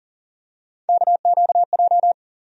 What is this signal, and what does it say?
Morse code, heard as a keyed tone, about 40 wpm, 700 Hz. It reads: KQJ
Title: USA Call Sign Postfixes - 3 Letters 40wpm